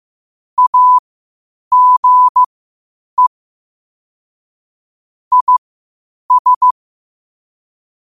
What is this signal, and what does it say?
Morse code, heard as a keyed tone, about 15 wpm, spaced as for 8 wpm, 1000 Hz. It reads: AGE IS
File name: qotdfiles / 8wpm